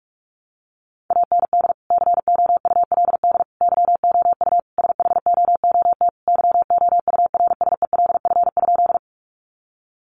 Morse code, heard as a keyed tone, about 44 wpm, 700 Hz. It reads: AND COULD YOU SHOOT YOURSELF?
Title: Sentences from Top 1200 Words 45wpm